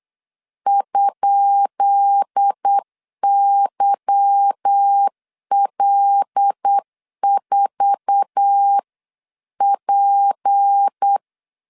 Morse code, heard as a keyed tone, about 8 wpm, 800 Hz. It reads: ?YL4P
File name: LEZIONI / 10 SIGNS / 10B INT 8wpm 3534